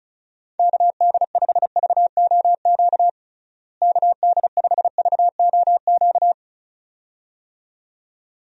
Morse code, heard as a keyed tone, about 34 wpm, 700 Hz. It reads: KD5VOQ KD5VOQ
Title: USA Call Signs - 6 Characters - Repeated 2x - 35wpm